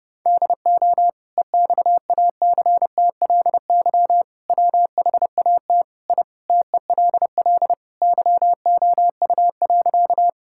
Morse code, heard as a keyed tone, about 30 wpm, 700 Hz. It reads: DO EXACTLY WHAT I TELL YOU.
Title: Sentences from Top 1000 Words - Review Part 6 - 30wpm